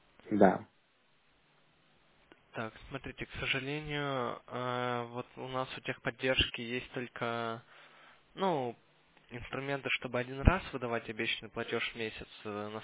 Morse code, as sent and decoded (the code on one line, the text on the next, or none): none